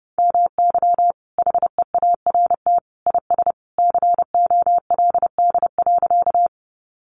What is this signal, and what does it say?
Morse code, heard as a keyed tone, about 30 wpm, 700 Hz. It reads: MY HEART IS COLD.
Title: Sentences from Top 1200 Words 30wpm